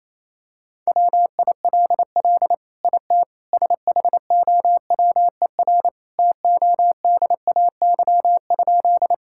Morse code, heard as a keyed tone, about 28 wpm, 700 Hz. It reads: WILL IT SHOWER TODAY?
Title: Sentences from Top 1600 Words 28wpm